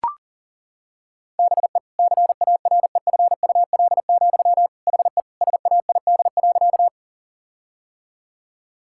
Morse code, heard as a keyed tone, about 37 wpm, 700 Hz. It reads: BE CAREFUL, HE SAID.
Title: Sentences from Top 1600 Words 40wpm